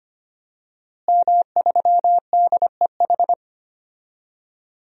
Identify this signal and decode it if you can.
Morse code, heard as a keyed tone, about 25 wpm, 700 Hz. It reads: M3DEH
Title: DXCC-100 Call Signs - 5 Characters - Repeated 2x - 25wpm